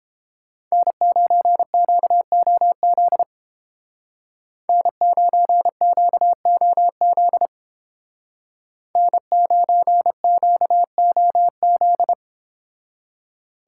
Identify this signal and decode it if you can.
Morse code, heard as a keyed tone, about 30 wpm, 700 Hz. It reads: N9QOZ N9QOZ N9QOZ